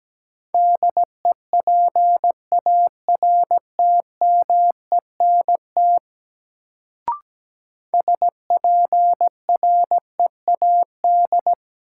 Morse code, heard as a keyed tone, about 17 wpm, 700 Hz. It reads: DEPARTMENT SPREAD
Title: Sentences from Top 1200 Words - New Word Drill Alt - 17wpm